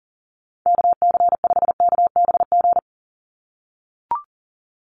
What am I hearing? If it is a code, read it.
Morse code, heard as a keyed tone, about 40 wpm, 690 Hz. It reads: KC5KBG
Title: USA Call Signs 40wpm